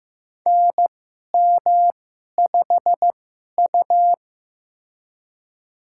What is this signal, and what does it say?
Morse code, heard as a keyed tone, about 15 wpm, spaced as for 11 wpm, 700 Hz. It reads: NM5U
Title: USA Call Signs - 4 Characters - 2x Character Spacing - 15wpm